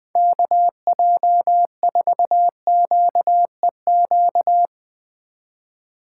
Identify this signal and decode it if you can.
Morse code, heard as a keyed tone, about 20 wpm, 700 Hz. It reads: KJ4QEQ